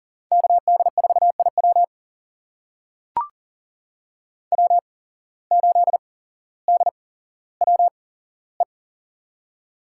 Morse code, heard as a keyed tone, about 39 wpm, spaced as for 17 wpm, 700 Hz. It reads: KD4IW W8DWE
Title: USA Call Signs - 5 Characters - 8x Character Spacing - 40wpm